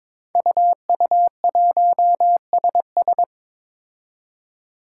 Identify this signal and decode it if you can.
Morse code, heard as a keyed tone, about 21 wpm, 700 Hz. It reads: UU1SS
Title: Five Challenging Letter-Number Combinations 22wpm